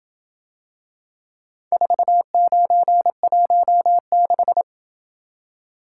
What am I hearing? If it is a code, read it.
Morse code, heard as a keyed tone, about 27 wpm, 700 Hz. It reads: 4916